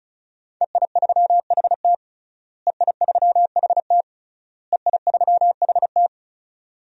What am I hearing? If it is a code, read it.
Morse code, heard as a keyed tone, about 34 wpm, 700 Hz. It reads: EI3HT EI3HT EI3HT
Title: DXCC-100 Call Signs - 5 Characters - Repeated 3x - 35wpm